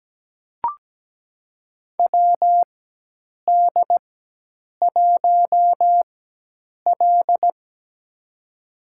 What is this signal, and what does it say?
Morse code, heard as a keyed tone, about 17 wpm, spaced as for 8 wpm, 700 Hz. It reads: WD1L
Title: USA Call Signs - 4 Characters - 4x Character Spacing - 17wpm